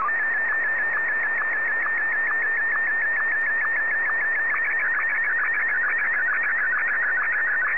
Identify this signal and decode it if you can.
SSTV (slow-scan television), Martin 1.
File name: sstv / M1-VK3